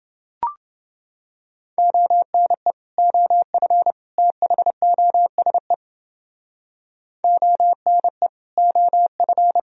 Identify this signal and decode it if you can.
Morse code, heard as a keyed tone, about 29 wpm, 700 Hz. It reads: ONE OF THOSE ONE OF